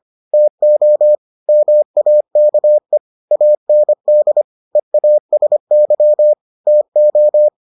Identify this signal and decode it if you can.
Morse code, heard as a keyed tone, about 25 wpm, 600 Hz. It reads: TO MAKE AND EASY TO